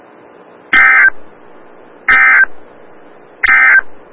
Emergency Alert System (SAME) tones, heard as harsh buzzing bursts. It is heard from 0.6 to 3.9 s.
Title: Des Moines, IA NOAA Weather Radio - WXL57